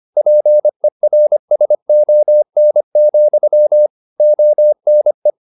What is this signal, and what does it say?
Morse code, heard as a keyed tone, about 25 wpm, 600 Hz. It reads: PERSON, ONE
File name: Never 0025wpm